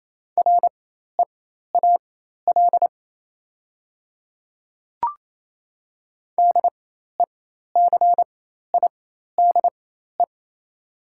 Morse code, heard as a keyed tone, about 28 wpm, spaced as for 13 wpm, 700 Hz. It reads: REAL DECIDE
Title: 4x Character Spacing - Top 500 Words - 28wpm